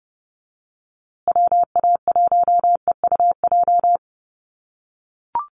Morse code, heard as a keyed tone, about 30 wpm, 700 Hz. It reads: WA1EUJ
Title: USA Call Signs Repeated 3x 30wpm